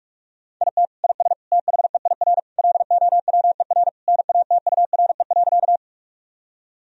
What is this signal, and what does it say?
Morse code, heard as a keyed tone, about 42 wpm, 700 Hz. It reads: IT IS THEIR LOWER NATURE.
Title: Sentences from Top 1000 Words - Review Part 4 - 45wpm